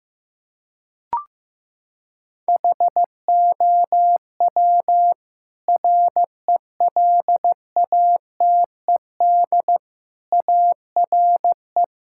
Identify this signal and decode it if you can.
Morse code, heard as a keyed tone, about 15 wpm, 700 Hz. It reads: HOW RELATED ARE
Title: Sentences from Top 1000 Words - Review Part 5 - 15wpm